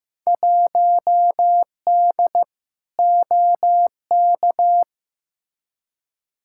Morse code, heard as a keyed tone, about 15 wpm, 700 Hz. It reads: 1D OK
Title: Field Day Exchanges - 15wpm